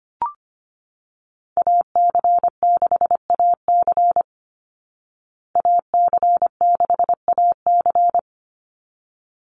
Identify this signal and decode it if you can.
Morse code, heard as a keyed tone, about 25 wpm, 700 Hz. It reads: AC6AC AC6AC
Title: Call Signs Repeated 3x - 25wpm